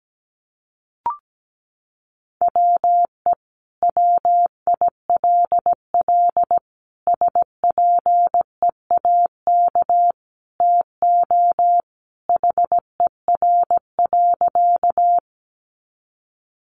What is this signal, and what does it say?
Morse code, heard as a keyed tone, about 17 wpm, 700 Hz. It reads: WE WILL SPEAK TO HER.
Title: Sentences from Top 1100 Words 17wpm